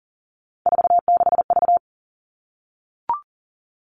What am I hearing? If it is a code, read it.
Morse code, heard as a keyed tone, about 40 wpm, 700 Hz. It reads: 46V
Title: All Cut Numbers - 3-Digit 40wpm